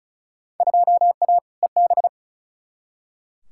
Morse code, heard as a keyed tone, about 36 wpm, 700 Hz. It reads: 2A EB
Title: Field Day Exchanges - 35wpm